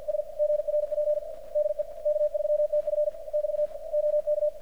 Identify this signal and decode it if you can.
Morse code, heard as a keyed tone, about 53 wpm, 600 Hz, with radio noise. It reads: S GRP DE K3TF <BT> OK
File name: archived / CRSnet